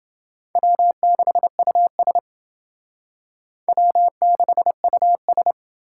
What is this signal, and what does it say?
Morse code, heard as a keyed tone, about 27 wpm, 700 Hz. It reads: W6US W6US